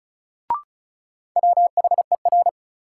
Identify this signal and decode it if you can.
Morse code, heard as a keyed tone, about 36 wpm, 700 Hz. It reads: WHER
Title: Sentences from Top 1500 Words - New Word Drill Alt - 35wpm